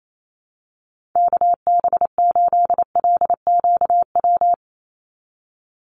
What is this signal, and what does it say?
Morse code, heard as a keyed tone, about 28 wpm, 700 Hz. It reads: KB8LQW
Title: USA Call Signs 28wpm